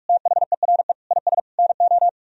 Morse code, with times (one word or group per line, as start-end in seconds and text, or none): 0.1-0.9 THERE
1.1-1.4 IS
1.6-2.1 NO